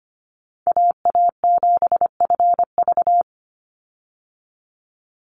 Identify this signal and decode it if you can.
Morse code, heard as a keyed tone, about 25 wpm, 700 Hz. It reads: AA7FV
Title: Call Signs Repeated 3x - 25wpm